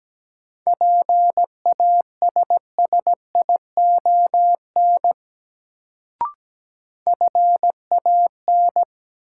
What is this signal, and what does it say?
Morse code, heard as a keyed tone, about 17 wpm, 700 Hz. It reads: PASSION FAN
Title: Sentences from Top 1300 Words - New Word Drill Alt - 17wpm